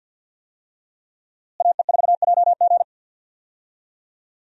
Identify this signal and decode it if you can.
Morse code, heard as a keyed tone, about 46 wpm, 700 Hz. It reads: AE4JG
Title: USA Call Signs - 5 Characters - Repeated 3x - 50wpm